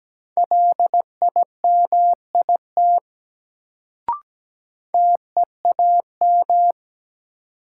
Morse code, heard as a keyed tone, about 17 wpm, 700 Hz. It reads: LIMIT TEAM
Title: Sentences from Top 1500 Words - New Word Drill Alt - 17wpm